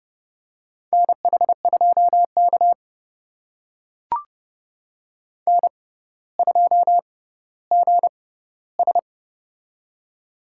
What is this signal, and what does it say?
Morse code, heard as a keyed tone, about 30 wpm, spaced as for 15 wpm, 700 Hz. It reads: NH2K N2GS
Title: USA Call Signs - 4 Characters - 6x Character Spacing - 30wpm